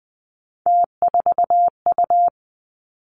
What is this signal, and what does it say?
Morse code, heard as a keyed tone, about 19 wpm, 700 Hz. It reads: T4U
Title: Three Challenging Letter-Number Combinations 20wpm